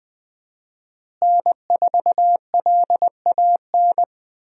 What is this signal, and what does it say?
Morse code, heard as a keyed tone, about 20 wpm, 700 Hz. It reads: N4LAN
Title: USA Call Signs - 5 Characters - Repeated 2x - 20wpm